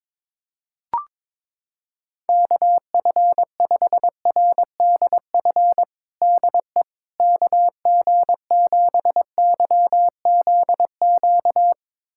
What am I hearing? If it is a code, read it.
Morse code, heard as a keyed tone, about 22 wpm, 700 Hz. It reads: KF5RDF DE KG7YZQ
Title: USA Callsign DE Callsign - 6 Characters - 22wpm